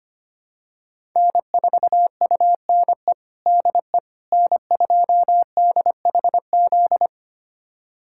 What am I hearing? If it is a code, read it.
Morse code, heard as a keyed tone, about 25 wpm, 700 Hz. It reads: N4UNE DE N2DHZ
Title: USA Callsign DE Callsign - 5 Characters - 25wpm